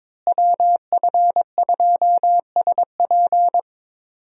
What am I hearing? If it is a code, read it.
Morse code, heard as a keyed tone, about 22 wpm, 700 Hz. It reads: WF2SP
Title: USA Call Signs - 5 Characters - Repeated 3x - 22wpm